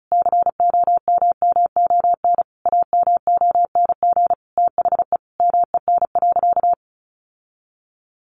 Morse code, heard as a keyed tone, about 34 wpm, 700 Hz. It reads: COMMON AMONG THE MEN.
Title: Sentences from Top 1400 Words 35wpm